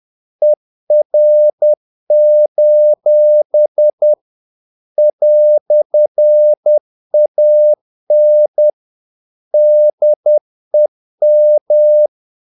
Morse code, heard as a keyed tone, about 10 wpm, 600 Hz. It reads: ER: "AN DEM
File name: dk5ke / Humor-50BpM